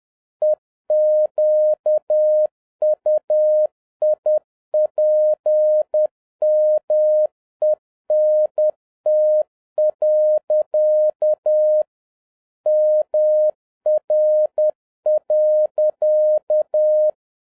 Morse code, heard as a keyed tone, about 10 wpm, 610 Hz. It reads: EQUIPMENT. MR.